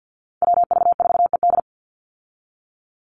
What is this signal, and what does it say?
Morse code, heard as a keyed tone, about 50 wpm, 700 Hz. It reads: WV4ED
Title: USA Call Signs 50wpm